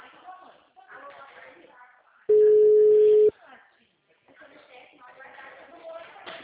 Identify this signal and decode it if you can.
Ringback tone (Europe, 425 Hz), with speech.